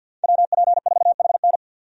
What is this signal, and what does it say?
Morse code, heard as a keyed tone, about 52 wpm, 700 Hz. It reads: WP4HN
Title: USA Call Signs - 5 Characters - 50wpm